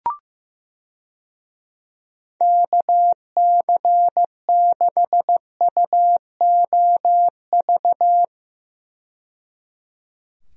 Morse code, heard as a keyed tone, about 15 wpm, 700 Hz. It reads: KC6UOV
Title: USA Call Signs - 6 Characters - 15wpm